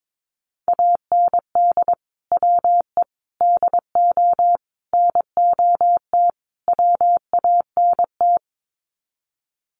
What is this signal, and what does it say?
Morse code, heard as a keyed tone, about 22 wpm, 700 Hz. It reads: AND WE DO NOT WANT
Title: Sets of 5 Words Encore2 22wpm